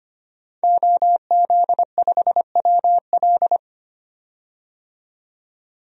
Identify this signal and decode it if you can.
Morse code, heard as a keyed tone, about 25 wpm, 700 Hz. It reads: OZ5WL
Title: DXCC-100 Call Signs - 5 Characters - 25wpm